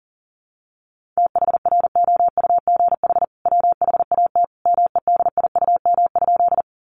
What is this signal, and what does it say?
Morse code, heard as a keyed tone, about 40 wpm, 700 Hz. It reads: THROUGH WHAT MEDIUM?
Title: Sentences from Top 900 Words 40wpm